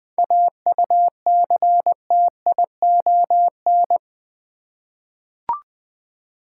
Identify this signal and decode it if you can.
Morse code, heard as a keyed tone, about 20 wpm, 700 Hz. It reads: AUCTION
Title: 7 Letter Words - First 3 Letters Given - 20wpm